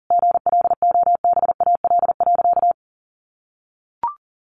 Morse code, heard as a keyed tone, about 40 wpm, 700 Hz. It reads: GLOBAL.
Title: Sentences from Top 1200 Words 40wpm